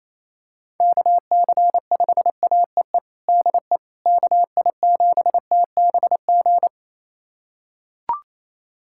Morse code, heard as a keyed tone, about 27 wpm, 700 Hz. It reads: KC5AEE DE KI7TBG
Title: USA Callsign DE Callsign - 6 Characters - 28wpm